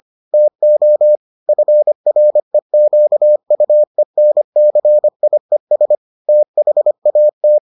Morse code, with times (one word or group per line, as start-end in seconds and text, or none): 0.3-1.2 TO
1.5-6.0 FREQUENCIES
6.3-7.6 THAT